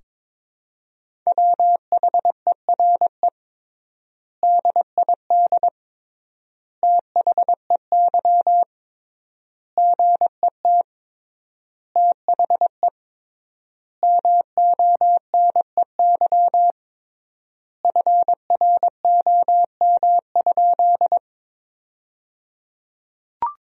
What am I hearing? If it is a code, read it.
Morse code, heard as a keyed tone, about 22 wpm, 700 Hz. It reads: WHERE DID THEY GET THE MONEY FROM?
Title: 3x Word-Spacing - Sentences from Top 400 Words - 22wpm